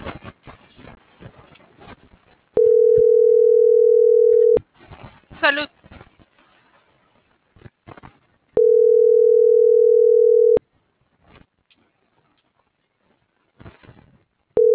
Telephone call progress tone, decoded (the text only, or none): ringback tone